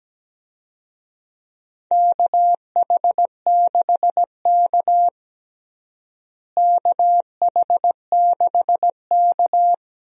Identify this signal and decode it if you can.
Morse code, heard as a keyed tone, about 17 wpm, 700 Hz. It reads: KH6K KH6K
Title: DXCC-100 Call Signs - 4 Characters - Repeated 3x - 17wpm